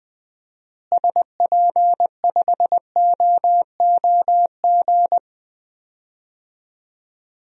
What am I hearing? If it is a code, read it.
Morse code, heard as a keyed tone, about 20 wpm, 700 Hz. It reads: SP5OOG
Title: DXCC-100 Call Signs - 6 Characters - Repeated 3x - 20wpm